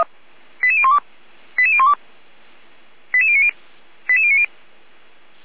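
Five-tone selective call (ZVEI), heard as quick runs of five tones, tones of about 80 ms. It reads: 88012 88012 88090 88090
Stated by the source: Received via SDR; ZVEI five-tone call 88451.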